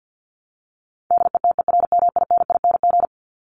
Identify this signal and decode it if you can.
Morse code, heard as a keyed tone, about 50 wpm, 700 Hz. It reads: DETERMINING
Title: ING Suffix - Top 100 Words with Base Words 50wpm